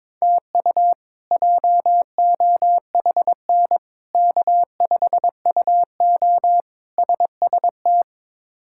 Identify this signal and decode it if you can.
Morse code, heard as a keyed tone, about 22 wpm, 700 Hz. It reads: TU JOHN K5UO SST
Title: K1USN SST QSO Practice - 22wpm